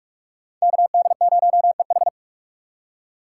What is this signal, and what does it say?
Morse code, heard as a keyed tone, about 44 wpm, 700 Hz. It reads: KD0EH
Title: USA Call Signs - 5 Characters - 3x Character Spacing - 45wpm